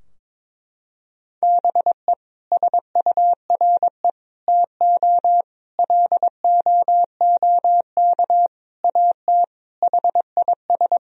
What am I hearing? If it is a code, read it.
Morse code, heard as a keyed tone, about 21 wpm, 700 Hz. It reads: BE SURE TO LOOK AT HIS
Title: Sentences from Top 1000 Words - Review Part 4 - 22wpm